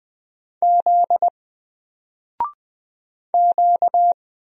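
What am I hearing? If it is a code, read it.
Morse code, heard as a keyed tone, about 20 wpm, 700 Hz. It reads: Z Q
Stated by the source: Single letter and number Morse drill at 20 wpm, from N4LCD.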